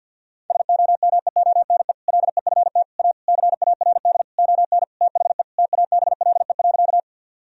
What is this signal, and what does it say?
Morse code, heard as a keyed tone, about 50 wpm, 700 Hz. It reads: SOMEONE LEFT A CARD ON THE TABLE.
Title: Sentences from Top 1000 Words - Review Part 4 - 50wpm